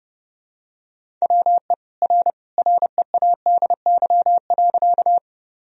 Morse code, heard as a keyed tone, about 30 wpm, 700 Hz. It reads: WE R READY.
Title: Sentences from Top 400 Words and Abbreviations 30wpm